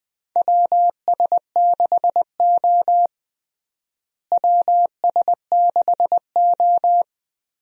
Morse code, heard as a keyed tone, about 20 wpm, 700 Hz. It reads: WS6O WS6O
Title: USA Call Signs - 4 Characters - Repeated 3x - 20wpm